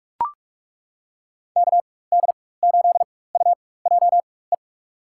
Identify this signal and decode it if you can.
Morse code, heard as a keyed tone, about 42 wpm, spaced as for 21 wpm, 700 Hz. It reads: KD8UJE
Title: USA Call Signs - 6 Characters - 4x Character Spacing - 45wpm